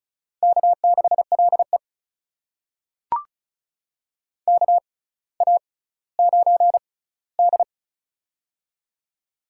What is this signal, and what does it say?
Morse code, heard as a keyed tone, about 36 wpm, spaced as for 18 wpm, 700 Hz. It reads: K6LE KA9D
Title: USA Call Signs - 4 Characters - 6x Character Spacing - 35wpm